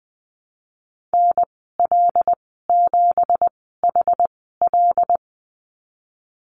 Morse code, heard as a keyed tone, about 19 wpm, spaced as for 14 wpm, 700 Hz. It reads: NL7HL
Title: 173 Lesson 28g - Number 9 - Call Signs 20wpm